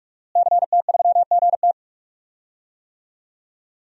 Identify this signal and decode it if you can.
Morse code, heard as a keyed tone, about 48 wpm, 700 Hz. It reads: CT3GT
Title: DXCC-100 Call Signs - 5 Characters - Repeated 2x - 45wpm